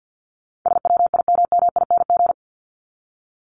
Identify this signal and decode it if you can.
Morse code, heard as a keyed tone, about 50 wpm, 700 Hz. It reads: SWIMMING
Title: Top 1501-1750 Words 50wpm